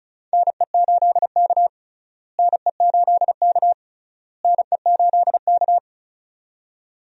Morse code, heard as a keyed tone, about 34 wpm, 700 Hz. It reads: NE8K NE8K NE8K
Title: USA Call Signs - 4 Characters - Repeated 3x - 35wpm